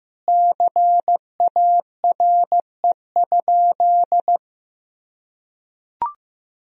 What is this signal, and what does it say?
Morse code, heard as a keyed tone, about 15 wpm, 700 Hz. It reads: CARE?
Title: Sentences from Top 1000 Words - Review Part 3 - 15wpm